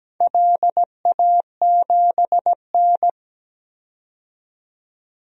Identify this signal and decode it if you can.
Morse code, heard as a keyed tone, about 17 wpm, 700 Hz. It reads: LA7N
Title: DXCC-100 Call Signs - 4 Characters - 17wpm